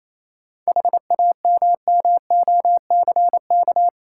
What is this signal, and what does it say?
Morse code, heard as a keyed tone, about 26 wpm, 690 Hz. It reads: HAMMOCK
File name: Easy Words - Speed-Racing with Hints - 22wpm